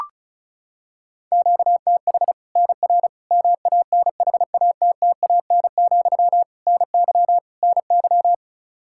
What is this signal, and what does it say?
Morse code, heard as a keyed tone, about 34 wpm, 700 Hz. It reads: QTH NR MANHATTAN, NY NY
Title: Practice QSO - Exchange 1 - Beginning - 35wpm